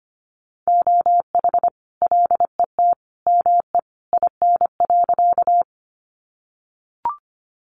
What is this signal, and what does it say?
Morse code, heard as a keyed tone, about 25 wpm, 700 Hz. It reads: OH LET ME IN.
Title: Sentences from Top 500 Words 25wpm